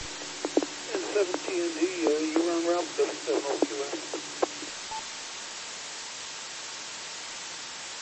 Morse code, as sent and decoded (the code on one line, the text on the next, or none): none